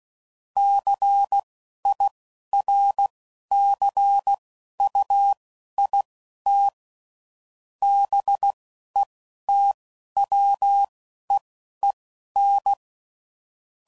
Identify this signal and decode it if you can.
Morse code, heard as a keyed tone, about 16 wpm, spaced as for 12 wpm, 800 Hz. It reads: CIRCUIT BETWEEN
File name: CW Code Groups @10WPM / Technical Subjects / SwPwrSup